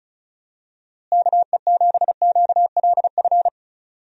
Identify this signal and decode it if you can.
Morse code, heard as a keyed tone, about 34 wpm, 700 Hz. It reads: KE7QLF
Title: USA Call Signs - 6 Characters - Repeated 2x - 35wpm